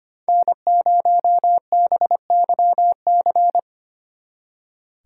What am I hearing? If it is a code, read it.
Morse code, heard as a keyed tone, about 30 wpm, 700 Hz. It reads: N0BYC
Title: Speed Racing - USA 1x3 Call Signs 25wpm